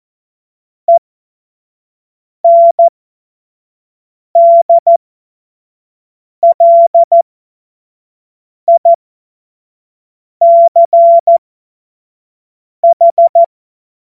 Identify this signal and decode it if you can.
Morse code, heard as a keyed tone, about 14 wpm, spaced as for 5 wpm, 680 Hz. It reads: ENDLICH
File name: Witz-16-September-2025-25-F70